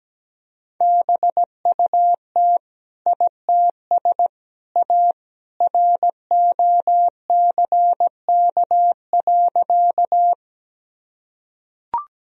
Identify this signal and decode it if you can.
Morse code, heard as a keyed tone, about 17 wpm, 700 Hz. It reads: BUT ITS A ROCK.
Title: Sentences from Top 1000 Words - Review Part 2 - 17wpm